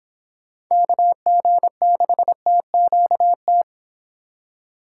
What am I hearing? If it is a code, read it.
Morse code, heard as a keyed tone, about 26 wpm, 700 Hz. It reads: KG6TQT
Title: Speed Racing - USA 2x3 Call Signs 22wpm